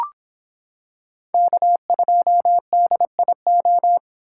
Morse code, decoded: K2DIO

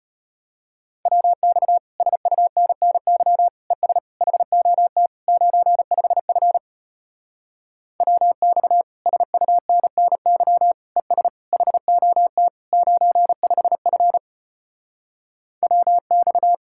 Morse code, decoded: WX SUNNY ES HOT 95F WX SUNNY ES HOT 95F WX

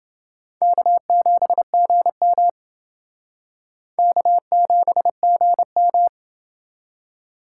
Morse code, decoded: K7GM K7GM

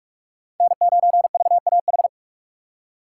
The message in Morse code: N9VAH

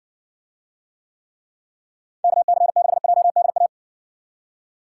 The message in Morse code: KC6JDA

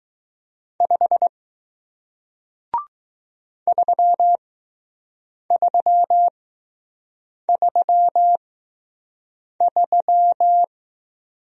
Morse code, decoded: 5 3333

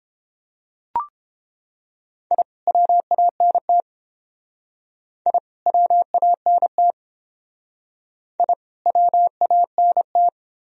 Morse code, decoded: I WANT I WANT I WANT

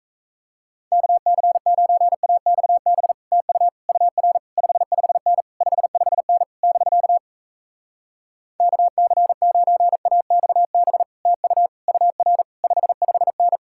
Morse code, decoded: KC9AXB TU UR 55N 55N <BK> KC9AXB TU UR 55N

